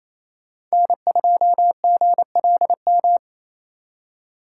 Morse code, heard as a keyed tone, about 33 wpm, 700 Hz. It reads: N2GLM
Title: Speed Racing - USA 1x3 Call Signs 28wpm